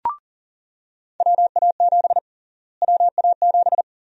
Morse code, heard as a keyed tone, about 40 wpm, 700 Hz. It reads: WA7 WA7
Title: USA Call Sign Prefixes - Repeated 2x - 40wpm